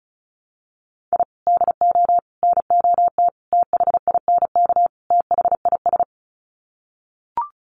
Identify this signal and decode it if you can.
Morse code, heard as a keyed tone, about 34 wpm, 700 Hz. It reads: I DO NOT THINK THIS
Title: Sets of 5 Words Encore 35wpm